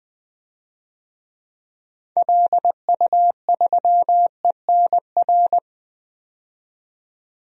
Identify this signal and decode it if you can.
Morse code, heard as a keyed tone, about 20 wpm, 700 Hz. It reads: LU3ENR
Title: DXCC-100 Call Signs - 6 Characters - 20wpm